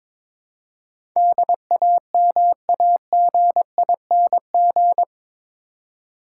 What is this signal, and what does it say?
Morse code, heard as a keyed tone, about 22 wpm, 700 Hz. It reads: DAMAGING
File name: practice / 22wpm / Eight Letter Words - 22wpm